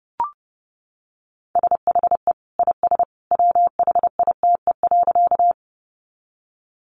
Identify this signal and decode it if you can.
Morse code, heard as a keyed tone, about 28 wpm, 700 Hz. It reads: SHE IS WHITE.
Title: Easy Phrases 30wpm